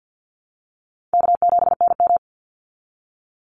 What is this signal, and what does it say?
Morse code, heard as a keyed tone, about 50 wpm, 700 Hz. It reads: K7NM
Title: Call Signs Repeated 3x - 50wpm